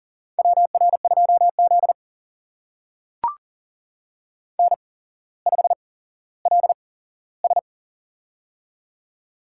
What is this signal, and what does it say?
Morse code, heard as a keyed tone, about 38 wpm, spaced as for 17 wpm, 700 Hz. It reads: WR2Z N5LS
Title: USA Call Signs - 4 Characters - 8x Character Spacing - 40wpm